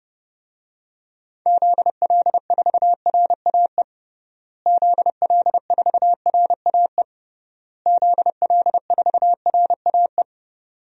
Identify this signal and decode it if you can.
Morse code, heard as a keyed tone, about 32 wpm, 700 Hz. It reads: ZL4RAE ZL4RAE ZL4RAE
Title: DXCC-100 Call Signs - 6 Characters - Repeated 3x - 30wpm